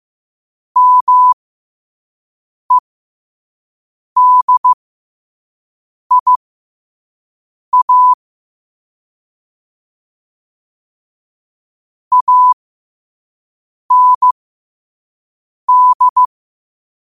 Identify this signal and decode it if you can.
Morse code, heard as a keyed tone, about 15 wpm, spaced as for 5 wpm, 1000 Hz. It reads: MEDIA AND
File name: qotdfiles / 5wpm